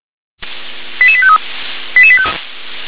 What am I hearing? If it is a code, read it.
Five-tone selective call (ZVEI), heard as quick runs of five tones, tones of about 70 ms. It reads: 88853
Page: SDR/ZVEI